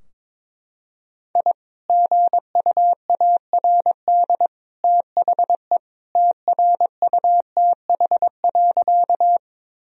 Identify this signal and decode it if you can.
Morse code, heard as a keyed tone, about 22 wpm, 700 Hz. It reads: I GUARD THE TRUTH.
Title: Sentences from Top 1000 Words - Review Part 4 - 22wpm